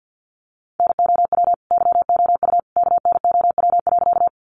Morse code, heard as a keyed tone, about 50 wpm, 700 Hz. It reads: NOW YOU KNOW.
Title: Sentences from Top 100 Words 50wpm